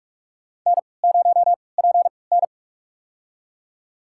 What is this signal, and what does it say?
Morse code, heard as a keyed tone, about 46 wpm, spaced as for 26 wpm, 700 Hz. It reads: N0PN
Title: USA Call Signs - 3x Character Spacing - 45wpm